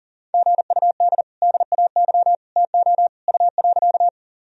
Morse code, heard as a keyed tone, about 40 wpm, 700 Hz. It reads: GUD DAY TO U.
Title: Sentences from Top 200 Words and Abbreviations 40wpm